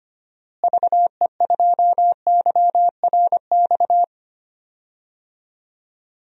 Morse code, heard as a keyed tone, about 25 wpm, 700 Hz. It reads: VE2YRX
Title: DXCC-100 Call Signs - 6 Characters - Repeated 3x - 25wpm